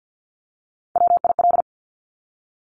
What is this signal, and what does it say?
Morse code, heard as a keyed tone, about 50 wpm, 700 Hz. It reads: WIL
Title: Top Three Letter Combinations in Words 50wpm